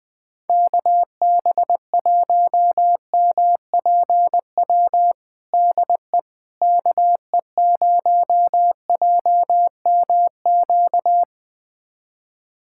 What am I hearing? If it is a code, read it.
Morse code, heard as a keyed tone, about 19 wpm, 700 Hz. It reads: KB1MPW DE KE0JMQ
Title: USA Callsign DE Callsign - 6 Characters - 20wpm